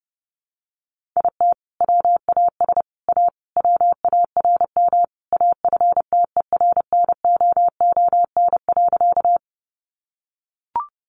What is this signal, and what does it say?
Morse code, heard as a keyed tone, about 30 wpm, 700 Hz. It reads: IT WAS A WARM AFTERNOON.
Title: Sentences from Top 1500 Words 30wpm